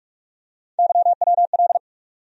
Morse code, heard as a keyed tone, about 46 wpm, 700 Hz. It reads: YWL
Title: USA Call Sign Postfixes - 3 Letters 45wpm